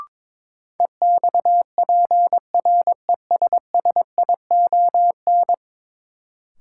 Morse code, heard as a keyed tone, about 22 wpm, 700 Hz. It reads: EXPRESSION
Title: Sentences from Top 1200 Words - New Word Drill Alt - 22wpm